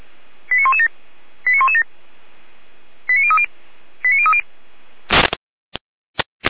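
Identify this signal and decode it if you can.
Five-tone selective call (ZVEI), heard as quick runs of five tones, tones of about 70 ms. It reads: 89117 89117 89030 89030
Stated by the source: ZVEI five-tone selective call received with an SDR.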